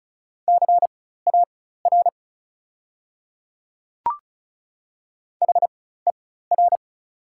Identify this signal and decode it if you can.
Morse code, heard as a keyed tone, about 36 wpm, spaced as for 16 wpm, 690 Hz. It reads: CAR HER
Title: 4x Character Spacing - Top 500 Words - 35wpm